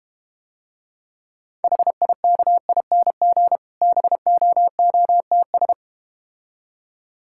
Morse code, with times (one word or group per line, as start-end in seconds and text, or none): 1.6-3.6 HIKING
3.8-5.7 BOOTS